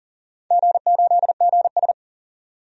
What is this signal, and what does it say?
Morse code, heard as a keyed tone, about 40 wpm, 700 Hz. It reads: G8GS
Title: DXCC-100 Call Signs - 4 Characters - 40wpm